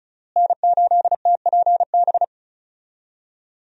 Morse code, heard as a keyed tone, about 34 wpm, 700 Hz. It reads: N8TPB
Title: USA Call Signs - 5 Characters - Repeated 3x - 35wpm